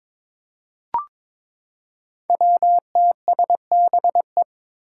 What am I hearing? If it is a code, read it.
Morse code, heard as a keyed tone, about 22 wpm, 700 Hz. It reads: WTSBE
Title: Five Challenging Letter-Number Combinations 22wpm